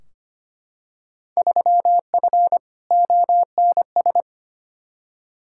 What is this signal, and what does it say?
Morse code, heard as a keyed tone, about 25 wpm, 700 Hz. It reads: 3F ONS
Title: Field Day Exchanges - 25wpm